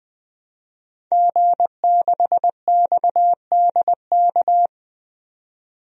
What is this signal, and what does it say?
Morse code, heard as a keyed tone, about 20 wpm, 700 Hz. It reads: G6XDK
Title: DXCC-100 Call Signs - 5 Characters - 20wpm